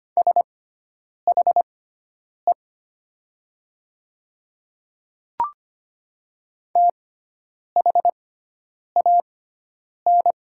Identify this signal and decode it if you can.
Morse code, heard as a keyed tone, about 25 wpm, spaced as for 9 wpm, 700 Hz. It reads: SHE THAN
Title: 6x Character Spacing - Top 100 Words - 25wpm